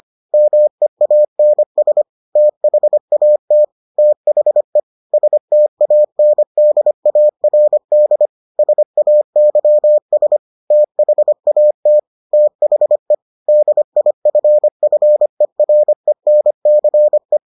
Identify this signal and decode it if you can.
Morse code, heard as a keyed tone, about 25 wpm, 600 Hz. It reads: MEANS THAT THE STANDARD SAYS THAT THE DIFFERENCE